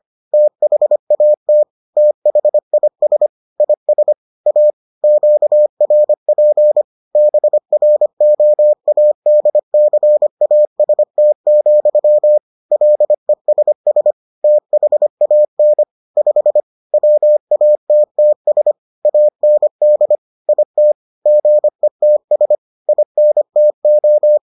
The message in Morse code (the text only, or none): THAT THIS IS A QRP BROADCAST, LESS THAN 5 WATTS AND IT GETS INTO